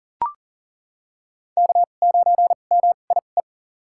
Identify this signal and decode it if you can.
Morse code, heard as a keyed tone, about 40 wpm, spaced as for 29 wpm, 700 Hz. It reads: K9MIE
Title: USA Call Signs - 2x Character Spacing - 40wpm